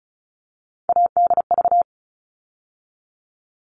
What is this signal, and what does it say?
Morse code, heard as a keyed tone, about 34 wpm, 700 Hz. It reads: ADV
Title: All Cut Numbers - 3-Digit 35wpm